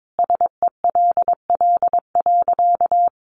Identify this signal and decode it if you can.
Morse code, heard as a keyed tone, about 22 wpm, 700 Hz. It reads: SELL.
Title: Sentences from Top 1100 Words 22wpm